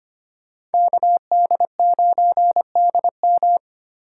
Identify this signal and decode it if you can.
Morse code, heard as a keyed tone, about 32 wpm, 700 Hz. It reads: KD9DM